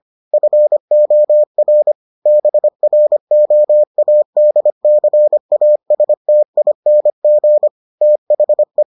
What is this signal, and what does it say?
Morse code, heard as a keyed tone, about 25 wpm, 600 Hz. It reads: FOR BROADCASTING THE